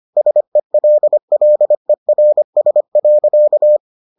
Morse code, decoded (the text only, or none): SELLERS.